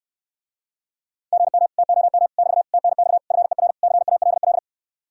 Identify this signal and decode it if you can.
Morse code, heard as a keyed tone, about 66 wpm, 700 Hz. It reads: GM TOM <BT> TNX FER CALL